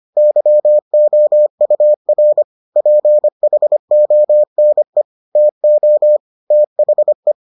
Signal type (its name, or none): Morse code